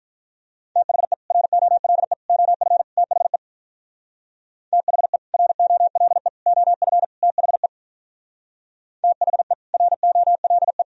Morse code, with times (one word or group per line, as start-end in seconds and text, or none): 0.8-1.1 THE
1.3-2.1 ROLE
2.3-2.8 OF
3.0-3.4 THE
4.7-5.2 THE
5.3-6.3 ROLE
6.5-7.0 OF
7.2-7.7 THE
9.0-9.5 THE
9.7-10.8 ROLE